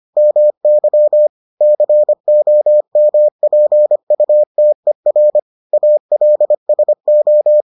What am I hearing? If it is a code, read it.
Morse code, heard as a keyed tone, about 25 wpm, 600 Hz. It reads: MY COMPUTER ALSO